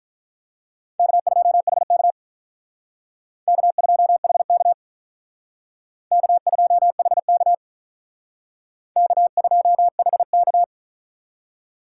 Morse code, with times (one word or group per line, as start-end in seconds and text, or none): 1.0-2.1 K2HK
3.5-4.7 K2HK
6.1-7.5 K2HK
9.0-10.6 K2HK